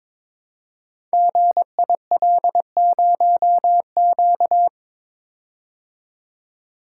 Morse code, heard as a keyed tone, about 22 wpm, 700 Hz. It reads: GIL0Q
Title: Code Groups - 5 Letter:Numbers - 22wpm